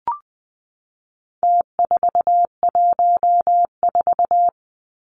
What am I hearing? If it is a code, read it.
Morse code, heard as a keyed tone, about 20 wpm, 700 Hz. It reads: T414
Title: Numbers - Contest Style - 4-Digit 20wpm